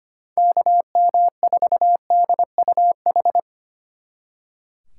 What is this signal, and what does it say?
Morse code, heard as a keyed tone, about 28 wpm, 700 Hz. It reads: KM4DUH
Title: Speed Racing - USA 2x3 Call Signs 25wpm